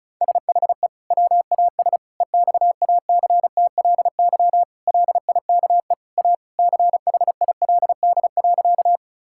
Morse code, heard as a keyed tone, about 36 wpm, 700 Hz. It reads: SHE WAS EXACTLY LIKE A CHILD.
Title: Sentences from Top 1000 Words - Review Part 6 - 35wpm